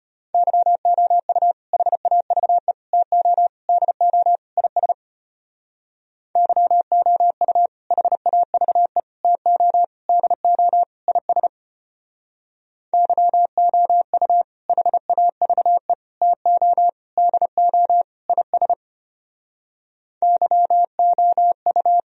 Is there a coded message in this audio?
Yes: Morse code from 0.3 to 22.0 s, heard as a keyed tone, about 32 wpm, 700 Hz. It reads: YOU HAVE TO DO IS YOU HAVE TO DO IS YOU HAVE TO DO IS YOU